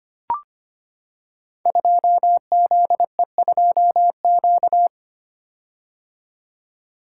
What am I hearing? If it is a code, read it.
Morse code, heard as a keyed tone, about 25 wpm, 700 Hz. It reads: 2ZE2Q
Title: Code Groups - 5 Letter:Numbers - 25wpm